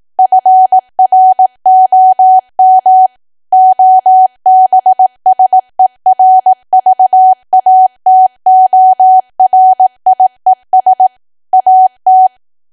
Morse code, heard as a keyed tone, about 18 wpm, 750 Hz. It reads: FROM OBSERVATORIES AT
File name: cwa / W1AW-Level2 / 18WPM